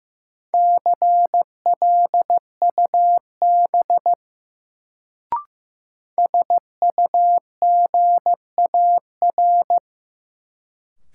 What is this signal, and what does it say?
Morse code, heard as a keyed tone, about 15 wpm, 700 Hz. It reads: CLUB SUGAR
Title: Sentences from Top 1200 Words - New Word Drill Alt - 15wpm